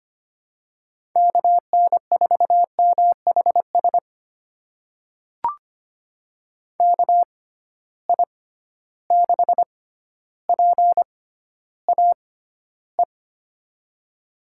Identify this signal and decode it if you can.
Morse code, heard as a keyed tone, about 25 wpm, spaced as for 13 wpm, 700 Hz. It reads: KN4MHS KI6PAE
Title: USA Call Signs - 6 Characters - 6x Character Spacing - 25wpm